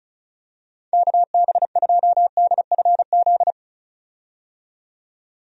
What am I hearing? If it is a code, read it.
Morse code, heard as a keyed tone, about 36 wpm, 700 Hz. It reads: KB2DFZ
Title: USA Call Signs - 6 Characters - 35wpm